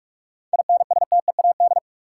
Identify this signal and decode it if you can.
Morse code, heard as a keyed tone, about 46 wpm, 700 Hz. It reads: INSTEAD
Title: Seven Letter Words - 45wpm